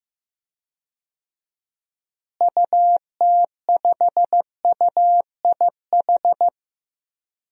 Morse code, heard as a keyed tone, about 15 wpm, 700 Hz. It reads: UT5UIH
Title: DXCC-100 Call Signs - 6 Characters - Repeated 2x - 15wpm